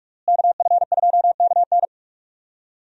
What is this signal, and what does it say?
Morse code, heard as a keyed tone, about 46 wpm, 700 Hz. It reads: KF2KN
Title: USA Call Signs - 5 Characters - Repeated 2x - 45wpm